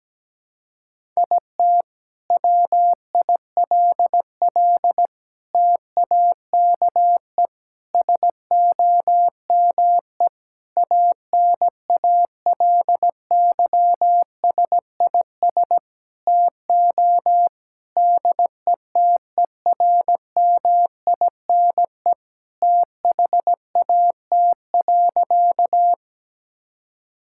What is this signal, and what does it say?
Morse code, heard as a keyed tone, about 17 wpm, 700 Hz. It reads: IT WILL TAKE SOME ANALYSIS TO DETERMINE THAT.
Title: Sentences from Top 1000 Words - Review Part 5 - 17wpm